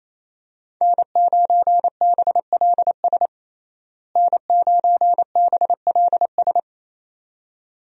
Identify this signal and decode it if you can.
Morse code, heard as a keyed tone, about 28 wpm, 700 Hz. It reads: N9BLS N9BLS
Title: USA Call Signs - 5 Characters - Repeated 2x - 28wpm